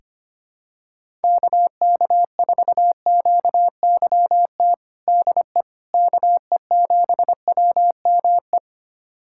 Morse code, heard as a keyed tone, about 25 wpm, 700 Hz. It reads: KK4QYT DE KE7WME
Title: USA Callsign DE Callsign - 6 Characters - 25wpm